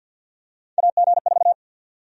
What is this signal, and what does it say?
Morse code, heard as a keyed tone, about 52 wpm, 700 Hz. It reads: AG4
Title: USA Call Sign Prefixes - 50wpm